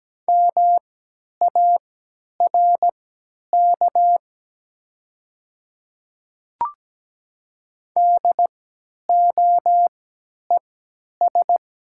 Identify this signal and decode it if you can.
Morse code, heard as a keyed tone, about 17 wpm, spaced as for 10 wpm, 700 Hz. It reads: MARK DOES
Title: 3x Character Spacing - Top 300 Words - 17wpm